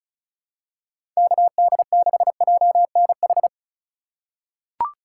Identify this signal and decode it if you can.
Morse code, heard as a keyed tone, about 34 wpm, 700 Hz. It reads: KD6JNH
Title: USA Call Signs - 6 Characters - 2x Character Spacing - 35wpm